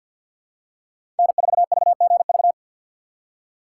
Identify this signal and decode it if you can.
Morse code, heard as a keyed tone, about 52 wpm, 700 Hz. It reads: N4VGV
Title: USA Call Signs - 4x Character Spacing - 50wpm